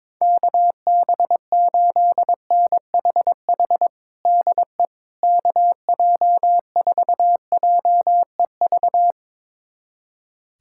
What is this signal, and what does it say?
Morse code, heard as a keyed tone, about 22 wpm, 700 Hz. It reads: KB8NHH DE KJ4JEV
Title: USA Callsign DE Callsign 22wpm